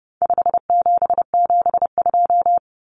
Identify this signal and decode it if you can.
Morse code, heard as a keyed tone, about 30 wpm, 700 Hz. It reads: 5772
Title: Common Cut Numbers - 4-Digit 30wpm